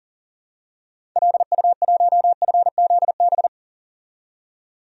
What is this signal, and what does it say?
Morse code, heard as a keyed tone, about 40 wpm, 700 Hz. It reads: LU1FZB
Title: DXCC-100 Call Signs - 6 Characters - 40wpm